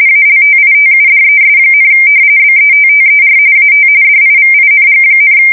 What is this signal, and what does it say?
RTTY, about 45.45 baud. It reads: ALTITUDES, BETTER AT DIVING,
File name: fsk170 / usnavy4